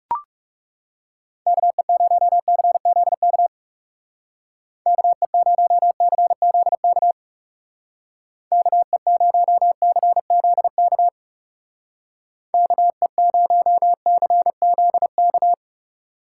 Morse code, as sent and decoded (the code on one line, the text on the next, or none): -.- . ----- -.-. --.. -.- / -.- . ----- -.-. --.. -.- / -.- . ----- -.-. --.. -.- / -.- . ----- -.-. --.. -.-
KE0CZK KE0CZK KE0CZK KE0CZK